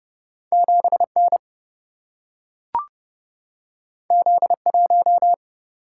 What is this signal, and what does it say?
Morse code, heard as a keyed tone, about 31 wpm, 700 Hz. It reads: 7N Z1
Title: Single-Letter Single-Number 30wpm